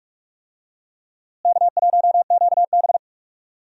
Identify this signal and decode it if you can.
Morse code, heard as a keyed tone, about 46 wpm, 700 Hz. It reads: K1QB
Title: USA Call Signs - 4 Characters - Repeated 2x - 45wpm